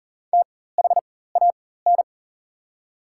Morse code, heard as a keyed tone, about 40 wpm, spaced as for 19 wpm, 700 Hz. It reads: THAN
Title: 4x Character Spacing - Top 100 Words - 40wpm